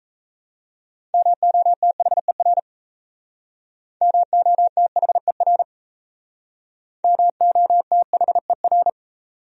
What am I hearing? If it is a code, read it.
Morse code, heard as a keyed tone, about 38 wpm, 700 Hz. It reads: MOTHER MOTHER MOTHER